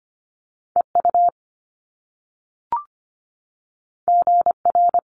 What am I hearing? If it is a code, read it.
Morse code, heard as a keyed tone, about 25 wpm, 700 Hz. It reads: EU GR
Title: Top Two Letter Combinations in Words 25wpm